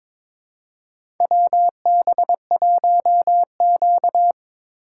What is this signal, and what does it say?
Morse code, heard as a keyed tone, about 22 wpm, 700 Hz. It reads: WB1Q
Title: USA Call Signs - 4 Characters - 2x Character Spacing - 22wpm